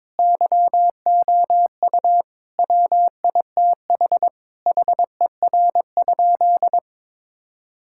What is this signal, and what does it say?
Morse code, heard as a keyed tone, about 22 wpm, 700 Hz. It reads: YOU WITH HER?
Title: Sentences from Top 1000 Words - Review Part 5 - 22wpm